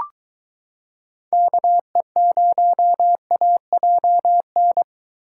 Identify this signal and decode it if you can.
Morse code, heard as a keyed tone, about 23 wpm, 700 Hz. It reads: KE0AJN